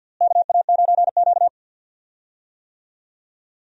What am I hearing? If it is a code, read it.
Morse code, heard as a keyed tone, about 48 wpm, 700 Hz. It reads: KA9Q
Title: USA Call Signs - 4 Characters - Repeated 2x - 50wpm